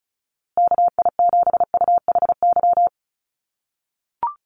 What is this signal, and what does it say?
Morse code, heard as a keyed tone, about 34 wpm, 700 Hz. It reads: KI7UHY
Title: USA Call Signs Repeated 2x 35wpm